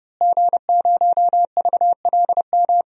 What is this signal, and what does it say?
Morse code, heard as a keyed tone, about 32 wpm, 700 Hz. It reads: G0VLM
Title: DXCC-100 Call Signs - 5 Characters - Repeated 2x - 30wpm